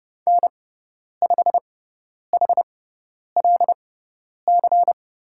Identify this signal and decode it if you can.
Morse code, heard as a keyed tone, about 30 wpm, spaced as for 10 wpm, 700 Hz. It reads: N5HLC
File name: Lesson 16g - Letter c - Call Signs 30wpm